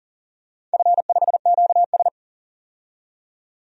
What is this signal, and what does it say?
Morse code, heard as a keyed tone, about 40 wpm, 700 Hz. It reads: F5QS
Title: DXCC-100 Call Signs - 4 Characters - 40wpm